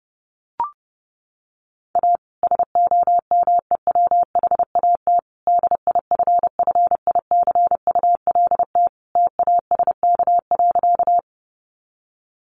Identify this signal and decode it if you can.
Morse code, heard as a keyed tone, about 30 wpm, 700 Hz. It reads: A SOMEWHAT DIFFICULT TASK.
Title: Sentences from Top 1300 Words 30wpm